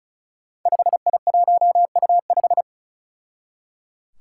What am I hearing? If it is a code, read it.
Morse code, heard as a keyed tone, about 36 wpm, 700 Hz. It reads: 5I1U5
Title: Five Challenging Letter-Number Combinations 35wpm